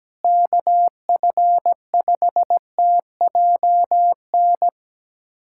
Morse code, heard as a keyed tone, about 17 wpm, 700 Hz. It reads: KF5TJN
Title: USA Callsign DE Callsign - 6 Characters - 17wpm